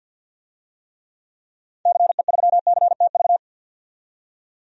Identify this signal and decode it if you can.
Morse code, heard as a keyed tone, about 50 wpm, 700 Hz. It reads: CE3CTV
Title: DXCC-100 Call Signs - 6 Characters - 50wpm